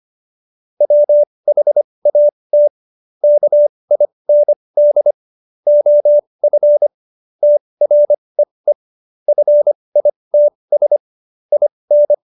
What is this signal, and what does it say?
Morse code, heard as a keyed tone, about 25 wpm, spaced as for 20 wpm, 590 Hz. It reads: WHAT KIND OF TREE FITS IN
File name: Jokes-25-char-20-eff